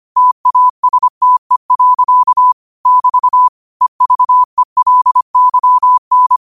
Morse code, heard as a keyed tone, about 25 wpm, 1000 Hz. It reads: TASTE. <BT> EVELYN
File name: qotdfiles / 25wpm